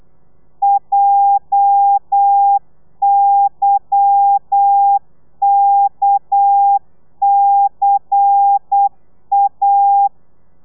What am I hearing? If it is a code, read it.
Morse code, heard as a keyed tone, about 8 wpm, 800 Hz. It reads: JYKCA
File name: telegrafia / i2tfj / Lez10 35